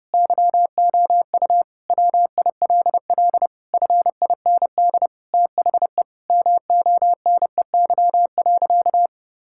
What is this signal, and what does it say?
Morse code, heard as a keyed tone, about 30 wpm, 700 Hz. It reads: YOU WILL FIND THE MONEY.
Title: Sentences from Top 1600 Words 30wpm